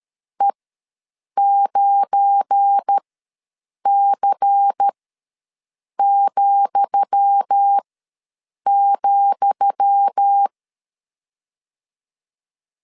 Morse code, heard as a keyed tone, about 13 wpm, spaced as for 7 wpm, 800 Hz. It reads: E9C,,
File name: LEZIONI / 10 SIGNS / 10D COMA ws 12wpm 3618